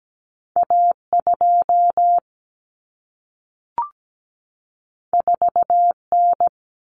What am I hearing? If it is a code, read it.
Morse code, heard as a keyed tone, about 17 wpm, 700 Hz. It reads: A2 4N
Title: Common Cut Numbers - 2-Digit 17wpm